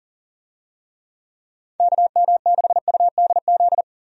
Morse code, heard as a keyed tone, about 40 wpm, 690 Hz. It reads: KM6UDZ
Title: USA Call Signs - 6 Characters - 40wpm